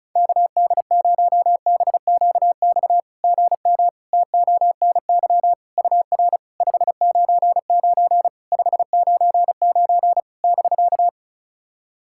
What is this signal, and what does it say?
Morse code, heard as a keyed tone, about 36 wpm, 700 Hz. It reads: KD0BQX GM TONY UR 599 599 <BK>
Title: SOTA QSO Practice 35wpm